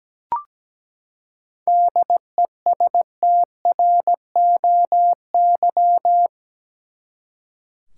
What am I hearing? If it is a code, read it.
Morse code, heard as a keyed tone, about 17 wpm, 700 Hz. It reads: DESTROY
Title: Seven Letter Words - 17wpm